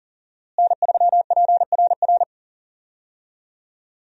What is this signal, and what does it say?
Morse code, heard as a keyed tone, about 39 wpm, 700 Hz. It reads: N3PRR
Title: USA Call Signs - 5 Characters - Repeated 3x - 40wpm